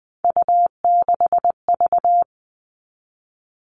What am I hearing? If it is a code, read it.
Morse code, heard as a keyed tone, about 20 wpm, 700 Hz. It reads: U6V
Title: All Cut Numbers - 3-Digit 20wpm